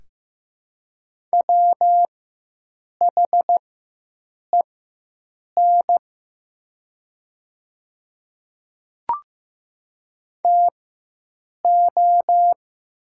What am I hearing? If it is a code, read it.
Morse code, heard as a keyed tone, about 15 wpm, spaced as for 7 wpm, 700 Hz. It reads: WHEN TO
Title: 4x Character Spacing - Top 100 Words - 15wpm